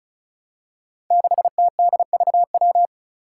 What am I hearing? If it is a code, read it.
Morse code, heard as a keyed tone, about 36 wpm, 700 Hz. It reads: 6TDVW
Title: Five Challenging Letter-Number Combinations 35wpm